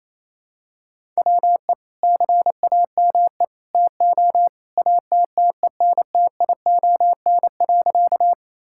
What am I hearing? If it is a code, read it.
Morse code, heard as a keyed tone, about 27 wpm, 700 Hz. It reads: WE CAME TO ATTENTION.
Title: Sentences from Top 1000 Words - Review Part 6 - 28wpm